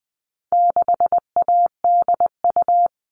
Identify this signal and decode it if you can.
Morse code, heard as a keyed tone, about 20 wpm, 700 Hz. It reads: 6ADU
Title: All Cut Numbers - 4-Digit 20wpm